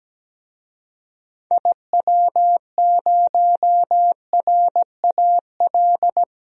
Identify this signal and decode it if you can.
Morse code, heard as a keyed tone, about 17 wpm, 700 Hz. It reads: IW0RAL
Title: DXCC-100 Call Signs - 6 Characters - 17wpm